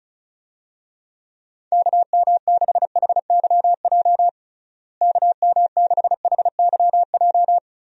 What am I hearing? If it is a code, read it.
Morse code, heard as a keyed tone, about 34 wpm, 700 Hz. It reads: KM6HYJ KM6HYJ
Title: USA Call Signs - 6 Characters - Repeated 3x - 35wpm